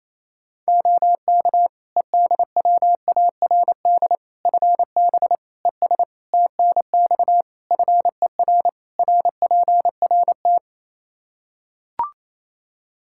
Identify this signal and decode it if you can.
Morse code, heard as a keyed tone, about 27 wpm, 700 Hz. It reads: OK EDWARD FB ES TNX FER RPRT
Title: Practice QSO - Exchange 2 - Info - 28wpm